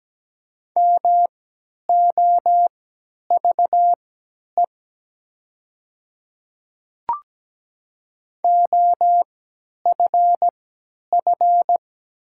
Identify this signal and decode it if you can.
Morse code, heard as a keyed tone, about 17 wpm, spaced as for 10 wpm, 700 Hz. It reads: MOVE OFF
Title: 3x Character Spacing - Top 200 Words - 17wpm